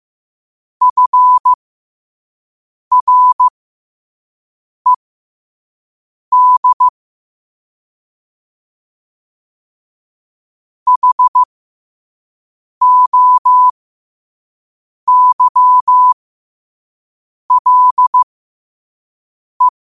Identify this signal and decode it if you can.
Morse code, heard as a keyed tone, about 15 wpm, spaced as for 5 wpm, 1000 Hz. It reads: FRED HOYLE
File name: qotdfiles / 5wpm